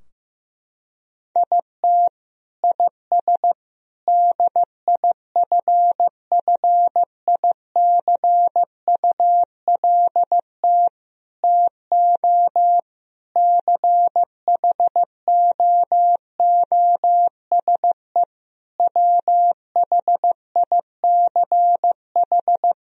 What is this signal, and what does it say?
Morse code, heard as a keyed tone, about 15 wpm, 700 Hz. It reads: IT IS DIFFICULT TO CHOOSE WHICH
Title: Sentences from Top 1000 Words - Review Part 3 - 15wpm